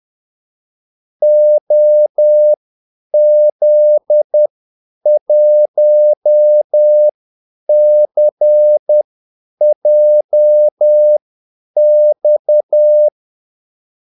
Morse code, heard as a keyed tone, about 10 wpm, spaced as for 8 wpm, 600 Hz. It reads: OZ1CJX
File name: Callsigns-10-8wpm